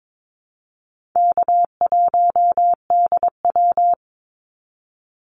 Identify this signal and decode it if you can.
Morse code, heard as a keyed tone, about 22 wpm, 700 Hz. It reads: K1DW
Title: Easy Call Signs - 22wpm